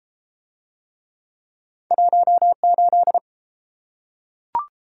Morse code, heard as a keyed tone, about 30 wpm, 700 Hz. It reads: 18